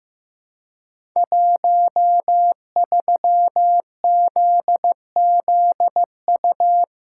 Morse code, heard as a keyed tone, about 15 wpm, 700 Hz. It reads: 13ZZU
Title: Code Groups - 5 Letter:Numbers - 15wpm